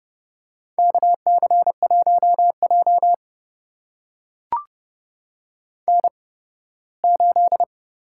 Morse code, heard as a keyed tone, about 30 wpm, spaced as for 18 wpm, 700 Hz. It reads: KC1J N8
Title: USA Call Signs - 8x Character Spacing - 30wpm